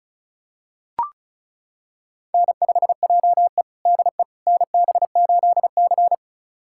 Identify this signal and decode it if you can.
Morse code, heard as a keyed tone, about 36 wpm, 700 Hz. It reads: N5JE DE NB8C
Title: USA Callsign DE Callsign - 4 Characters - 35wpm